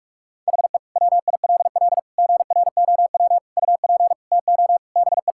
Morse code, heard as a keyed tone, about 41 wpm, 700 Hz. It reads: HE WILL GROW UP TO BE
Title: Sentences from Top 1000 Words - Review Part 6 - 45wpm